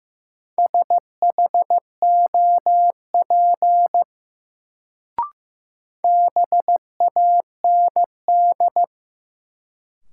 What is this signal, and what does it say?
Morse code, heard as a keyed tone, about 15 wpm, 700 Hz. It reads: SHOP BAND
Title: Sentences from Top 1600 Words - New Word Drill Alt - 15wpm